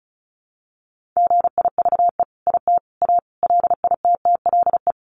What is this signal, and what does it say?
Morse code, heard as a keyed tone, about 34 wpm, 700 Hz. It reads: GIVE IT A LITTLE
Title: Sentences from Top 1500 Words 35wpm